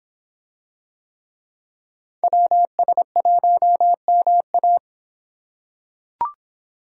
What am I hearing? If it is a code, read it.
Morse code, heard as a keyed tone, about 25 wpm, 700 Hz. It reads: WS1MA